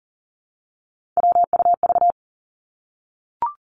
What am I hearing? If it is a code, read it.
Morse code, heard as a keyed tone, about 38 wpm, 700 Hz. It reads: WUV
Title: Three Challenging Letter-Number Combinations 40wpm